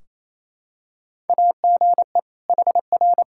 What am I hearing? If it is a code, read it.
Morse code, heard as a keyed tone, about 28 wpm, 700 Hz. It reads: AGE HR
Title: K1USN SST QSO Practice 28wpm